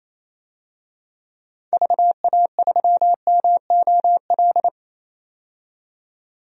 Morse code, heard as a keyed tone, about 28 wpm, 700 Hz. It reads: VA3MOL
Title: DXCC-100 Call Signs - 6 Characters - 28wpm